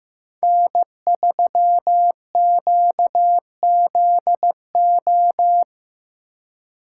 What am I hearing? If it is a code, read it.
Morse code, heard as a keyed tone, about 17 wpm, 700 Hz. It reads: N3QZO
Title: Speed Racing - USA 1x3 Call Signs 15wpm